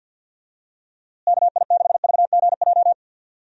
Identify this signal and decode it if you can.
Morse code, heard as a keyed tone, about 46 wpm, 700 Hz. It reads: KI6VGJ
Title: USA Call Signs - 6 Characters - Repeated 2x - 50wpm